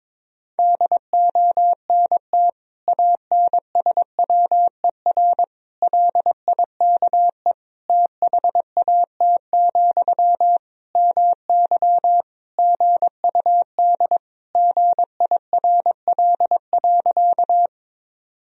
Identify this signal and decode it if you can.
Morse code, heard as a keyed tone, about 22 wpm, 700 Hz. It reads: DONT ANSWER LIKE THAT, MY GUD GIRL.
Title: Sentences from Top 400 Words and Abbreviations 22wpm